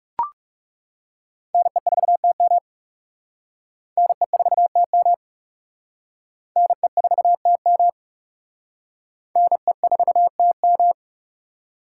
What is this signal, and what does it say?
Morse code, heard as a keyed tone, about 37 wpm, 700 Hz. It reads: NE4TM NE4TM NE4TM NE4TM
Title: Speed Racing - USA 2x2 Call Signs 30wpm